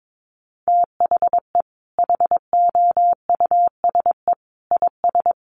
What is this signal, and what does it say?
Morse code, heard as a keyed tone, about 22 wpm, 700 Hz. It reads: THE HOUSE IS
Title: Sentences from Top 1200 Words 22wpm